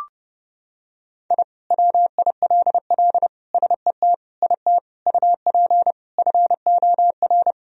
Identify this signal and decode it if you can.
Morse code, heard as a keyed tone, about 30 wpm, 700 Hz. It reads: I WILL SET IT UP FOR
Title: Sentences from Top 1000 Words - Review Part 1 - 30wpm